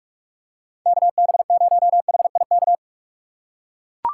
Morse code, heard as a keyed tone, about 39 wpm, 700 Hz. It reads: KB0HIK